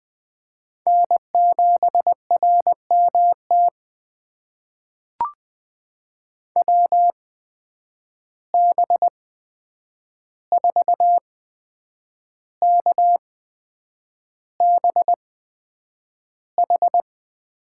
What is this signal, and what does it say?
Morse code, heard as a keyed tone, about 20 wpm, spaced as for 8 wpm, 700 Hz. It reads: N7RMT WB4KBH
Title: USA Call Signs - 8x Character Spacing - 20wpm